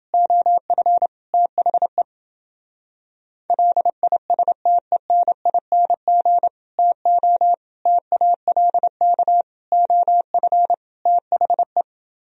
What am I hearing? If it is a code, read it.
Morse code, heard as a keyed tone, about 29 wpm, 700 Hz. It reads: OF THE LISTENING TO TALK OF THE